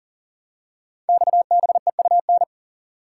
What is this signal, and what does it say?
Morse code, heard as a keyed tone, about 42 wpm, 700 Hz. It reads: XBEUN
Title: Code Groups - 5 Letters - 40wpm